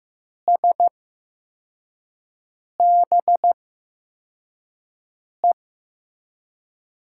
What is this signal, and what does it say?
Morse code, heard as a keyed tone, about 15 wpm, spaced as for 4 wpm, 700 Hz. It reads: SBE
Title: USA Call Signs - 5 Characters - 8x Character Spacing - 15wpm